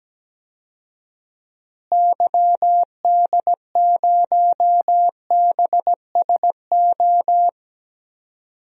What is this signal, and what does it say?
Morse code, heard as a keyed tone, about 17 wpm, 700 Hz. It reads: YD0BSO
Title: DXCC-100 Call Signs - 6 Characters - Repeated 3x - 17wpm